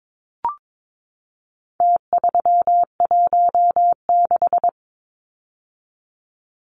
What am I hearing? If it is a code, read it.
Morse code, heard as a keyed tone, about 22 wpm, 700 Hz. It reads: T316
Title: Numbers - Contest Style - 4-Digit 22wpm